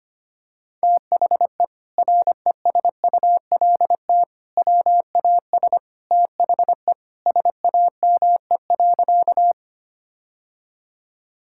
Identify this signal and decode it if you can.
Morse code, heard as a keyed tone, about 25 wpm, 700 Hz. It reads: THE RESULT WAS THE SAME.
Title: Sentences from Top 1000 Words - Review Part 5 - 25wpm